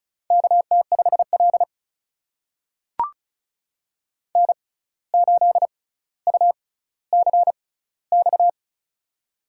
Morse code, heard as a keyed tone, about 36 wpm, spaced as for 17 wpm, 700 Hz. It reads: KT5L N8UCX
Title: USA Call Signs - 6x Character Spacing - 35wpm